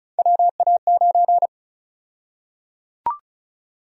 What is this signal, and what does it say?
Morse code, heard as a keyed tone, about 36 wpm, 700 Hz. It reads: WA9
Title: USA Call Sign Prefixes - Repeated 2x - 35wpm